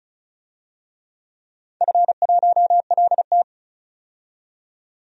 Morse code, heard as a keyed tone, about 36 wpm, 700 Hz. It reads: F1LT
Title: DXCC-100 Call Signs - 4 Characters - 35wpm